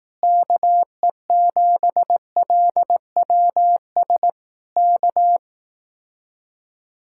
Morse code, heard as a keyed tone, about 18 wpm, 700 Hz. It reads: KE7LWS K
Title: Speed-Racing Common QSO Phrases Encore - 15wpm